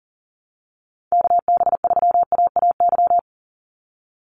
Morse code, heard as a keyed tone, about 38 wpm, 700 Hz. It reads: KB3AAY
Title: Call Signs Repeated 2x - 40wpm